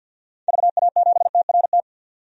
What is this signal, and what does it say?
Morse code, heard as a keyed tone, about 46 wpm, 700 Hz. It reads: VA7TRT
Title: DXCC-100 Call Signs - 6 Characters - 50wpm